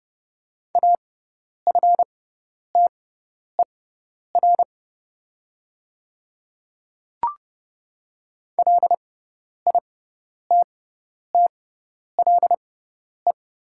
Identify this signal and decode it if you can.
Morse code, heard as a keyed tone, about 30 wpm, spaced as for 10 wpm, 700 Hz. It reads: AFTER LITTLE
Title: 6x Character Spacing - Top 200 Words - 30wpm